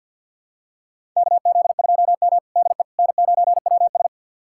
Morse code, heard as a keyed tone, about 51 wpm, 700 Hz. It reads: KZ2M DE N9WS
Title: USA Callsign DE Callsign - 4 Characters - 50wpm